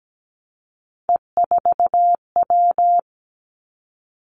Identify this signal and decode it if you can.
Morse code, heard as a keyed tone, about 17 wpm, 700 Hz. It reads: E4W
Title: Three Challenging Letter-Number Combinations 17wpm